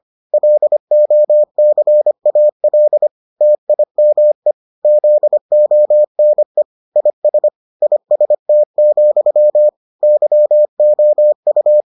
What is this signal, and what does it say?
Morse code, heard as a keyed tone, about 25 wpm, 600 Hz. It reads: LOCAL TIME ZONE IS IST, YOU